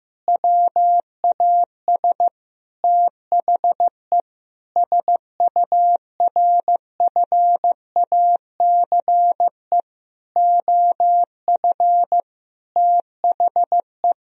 Morse code, heard as a keyed tone, about 15 wpm, 700 Hz. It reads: WAS THE SURFACE OF THE
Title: Sentences from Top 1000 Words - Review Part 3 - 15wpm